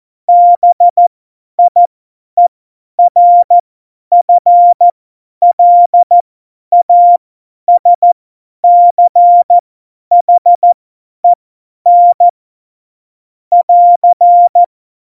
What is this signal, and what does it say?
Morse code, heard as a keyed tone, about 14 wpm, spaced as for 10 wpm, 700 Hz. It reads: BIERFLASCHEN <AR>
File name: Morsekurs / Witze / Witz 22. Oktober 2019, 50 F70